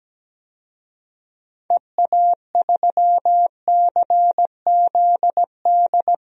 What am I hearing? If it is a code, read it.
Morse code, heard as a keyed tone, about 17 wpm, 700 Hz. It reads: EA3CZD
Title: DXCC-100 Call Signs - 6 Characters - 17wpm